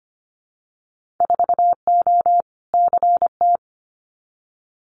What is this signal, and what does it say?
Morse code, heard as a keyed tone, about 25 wpm, 700 Hz. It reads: 4O CT
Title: Winter Field Day Exchanges Encore 25wpm